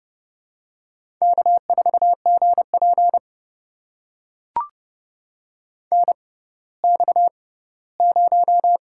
Morse code, heard as a keyed tone, about 30 wpm, spaced as for 17 wpm, 700 Hz. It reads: K4GP NX0
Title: USA Call Signs - 4 Characters - 6x Character Spacing - 30wpm